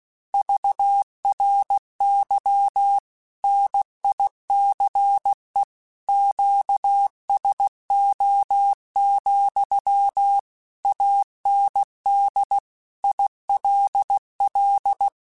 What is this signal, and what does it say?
Morse code, heard as a keyed tone, about 16 wpm, 800 Hz. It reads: VRY NICE QSO, AND ILL
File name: QSOs/QSOs-mp3s/kc7haa@15